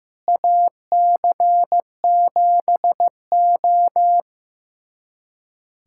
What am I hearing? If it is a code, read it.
Morse code, heard as a keyed tone, about 15 wpm, 700 Hz. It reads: AC7O
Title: USA Callsign DE Callsign - 4 Characters - 15wpm